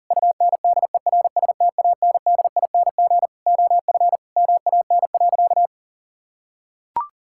Morse code, heard as a keyed tone, about 40 wpm, 700 Hz. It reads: UNDERSTANDING OF MAN.
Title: Sentences from Top 1000 Words - Review Part 5 - 40wpm